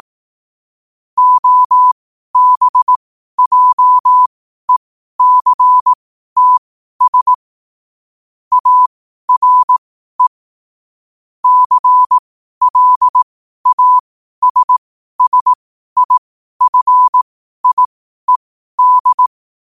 Morse code, heard as a keyed tone, about 18 wpm, spaced as for 12 wpm, 1000 Hz. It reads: OBJECTS ARE CLASSIFIED